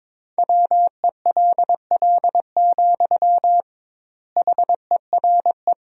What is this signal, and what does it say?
Morse code, heard as a keyed tone, about 22 wpm, 700 Hz. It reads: WELL, HERE
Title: 2x Word-Spacing - Sentences from Top 300 Words - 22wpm